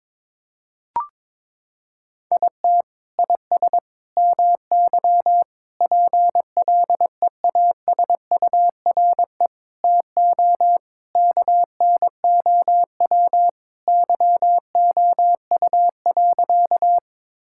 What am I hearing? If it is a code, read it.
Morse code, heard as a keyed tone, about 22 wpm, 700 Hz. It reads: IT IS MY PLEASURE TO KNOW YOU.
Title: Sentences from Top 1600 Words 22wpm